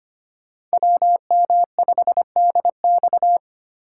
Morse code, heard as a keyed tone, about 25 wpm, 700 Hz. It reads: WM5DX
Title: K1USN SST QSO Practice - 25wpm